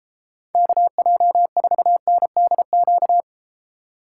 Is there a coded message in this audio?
Yes: Morse code from 0.5 to 3.2 s, heard as a keyed tone, about 31 wpm, 700 Hz. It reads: KJ4NDQ